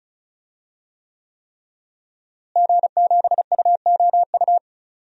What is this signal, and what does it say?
Morse code, heard as a keyed tone, about 36 wpm, 700 Hz. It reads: G7UOU
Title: DXCC-100 Call Signs - 5 Characters - Repeated 3x - 35wpm